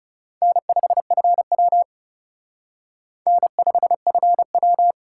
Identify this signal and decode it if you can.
Morse code, heard as a keyed tone, about 37 wpm, 700 Hz. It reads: N5FW N5FW
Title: Speed Racing - USA 1x2 Call Signs 30wpm